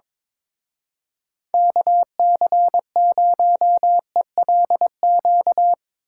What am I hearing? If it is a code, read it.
Morse code, heard as a keyed tone, about 24 wpm, 700 Hz. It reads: KC0ELQ
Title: Speed Racing - USA 2x3 Call Signs 22wpm